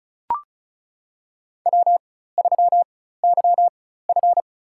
Morse code, heard as a keyed tone, about 34 wpm, spaced as for 16 wpm, 700 Hz. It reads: W3YF
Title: USA Call Signs - 4 Characters - 4x Character Spacing - 35wpm